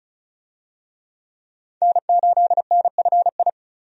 Morse code, heard as a keyed tone, about 36 wpm, 700 Hz. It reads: N8NFI
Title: USA Call Signs - 5 Characters - 35wpm